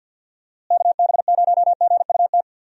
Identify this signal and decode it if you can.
Morse code, heard as a keyed tone, about 48 wpm, 700 Hz. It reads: KB0GUT
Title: USA Call Signs - 6 Characters - Repeated 3x - 50wpm